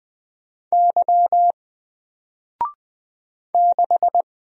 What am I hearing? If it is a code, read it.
Morse code, heard as a keyed tone, about 20 wpm, 700 Hz. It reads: Y 6
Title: Single Letter-Number Alt 20wpm